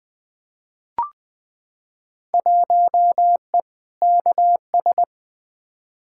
Morse code, heard as a keyed tone, about 20 wpm, 700 Hz. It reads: 1E KS
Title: Field Day Exchanges - 20wpm